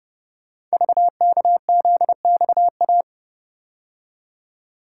Morse code, heard as a keyed tone, about 28 wpm, 700 Hz. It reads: VKZXA
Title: Code Groups - 5 Letters - 30wpm